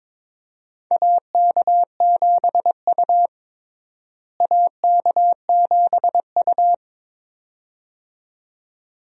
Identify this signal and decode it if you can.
Morse code, heard as a keyed tone, about 22 wpm, 700 Hz. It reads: AK7U AK7U
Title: USA Call Signs - 4 Characters - Repeated 2x - 22wpm